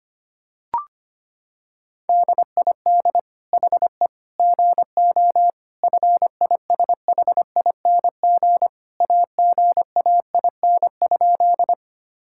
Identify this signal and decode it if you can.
Morse code, heard as a keyed tone, about 25 wpm, 700 Hz. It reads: DID HE GO FISHING AGAIN?
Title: Sentences from Top 1000 Words - Review Part 5 - 25wpm